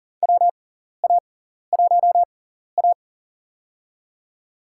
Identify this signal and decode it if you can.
Morse code, heard as a keyed tone, about 37 wpm, spaced as for 14 wpm, 700 Hz. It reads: WA1A
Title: USA Call Signs - 4 Characters - 6x Character Spacing - 40wpm